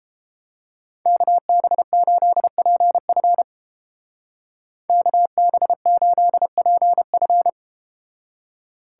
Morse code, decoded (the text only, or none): KB8PF KB8PF